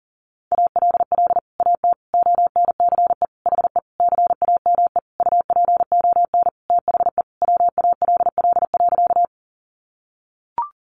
Morse code, heard as a keyed tone, about 40 wpm, 700 Hz. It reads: ALL AT ONCE HE CAME UPON THE WALL.
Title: Sentences from Top 1100 Words 40wpm